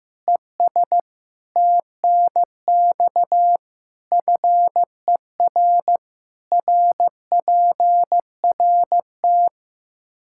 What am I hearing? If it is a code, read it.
Morse code, heard as a keyed tone, about 15 wpm, 700 Hz. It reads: ES TNX FER RPRT
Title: Practice QSO - Exchange 2 - Info - 15wpm